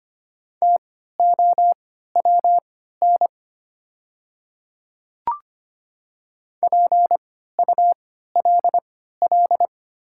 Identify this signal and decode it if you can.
Morse code, heard as a keyed tone, about 25 wpm, spaced as for 14 wpm, 700 Hz. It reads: TOWN PULL
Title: 3x Character Spacing - Top 500 Words - 25wpm